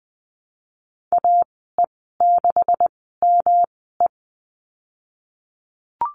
Morse code, heard as a keyed tone, about 20 wpm, spaced as for 15 wpm, 700 Hz. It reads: AE6ME
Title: 193 Lesson 31g - Letter v - Call Signs 20wpm